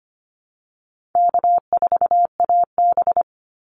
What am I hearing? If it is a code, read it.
Morse code, heard as a keyed tone, about 25 wpm, 700 Hz. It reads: K4AB
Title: Call Signs Repeated 2x - 25wpm